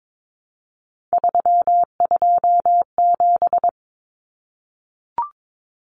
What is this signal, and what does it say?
Morse code, heard as a keyed tone, about 22 wpm, 700 Hz. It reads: 327
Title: Common Cut Numbers - 3-Digit 22wpm